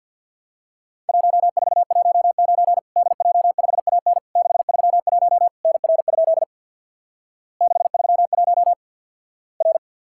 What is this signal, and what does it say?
Morse code, heard as a keyed tone, about 50 wpm, 700 Hz. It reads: 1419 DJ5AN 631 NR? 631 R
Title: Worked All Europe DX Contest - QTC Exchange with Repeats - 50wpm